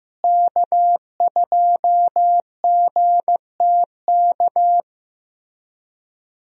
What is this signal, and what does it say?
Morse code, heard as a keyed tone, about 15 wpm, 700 Hz. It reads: K2GTK
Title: USA Call Signs - 5 Characters - Repeated 3x - 15wpm